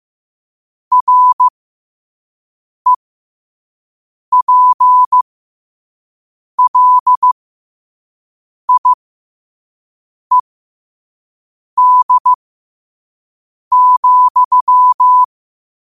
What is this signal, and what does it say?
Morse code, heard as a keyed tone, about 15 wpm, spaced as for 5 wpm, 1000 Hz. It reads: REPLIED,
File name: qotdfiles / 5wpm